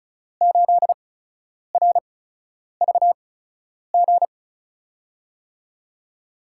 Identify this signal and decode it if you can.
Morse code, heard as a keyed tone, about 36 wpm, spaced as for 10 wpm, 700 Hz. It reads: 8RVG
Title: USA Call Signs - 6 Characters - 8x Character Spacing - 35wpm